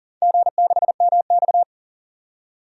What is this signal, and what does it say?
Morse code, heard as a keyed tone, about 37 wpm, 700 Hz. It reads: G6MX
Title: DXCC-100 Call Signs - 4 Characters - 40wpm